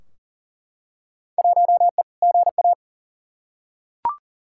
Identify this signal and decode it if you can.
Morse code, heard as a keyed tone, about 40 wpm, 700 Hz. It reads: 1E GA
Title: Field Day Exchanges - 40wpm